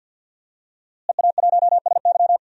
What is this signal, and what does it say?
Morse code, heard as a keyed tone, about 46 wpm, 700 Hz. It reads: EA1SY
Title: DXCC-100 Call Signs - 5 Characters - Repeated 3x - 50wpm